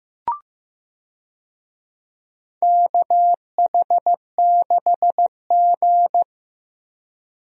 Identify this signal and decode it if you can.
Morse code, heard as a keyed tone, about 15 wpm, 700 Hz. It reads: KH6G
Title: DXCC-100 Call Signs - 4 Characters - 15wpm